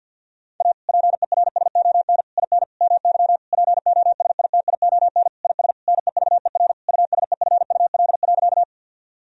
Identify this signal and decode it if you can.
Morse code, heard as a keyed tone, about 50 wpm, 700 Hz. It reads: A PERSON IN MY POSITION IS NEVER USEFUL.
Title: Sentences from Top 1000 Words - Review Part 4 - 50wpm